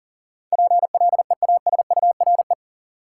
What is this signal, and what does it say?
Morse code, heard as a keyed tone, about 40 wpm, 700 Hz. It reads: PLEASURE
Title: Sentences from Top 1600 Words - New Word Drill - 40wpm